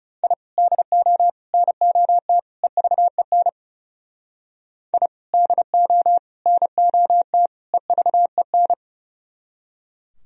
Morse code, heard as a keyed tone, about 35 wpm, 700 Hz. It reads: I DO NOT EVEN I DO NOT EVEN
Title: Speed-Racing - Sets of 4 Words 30wpm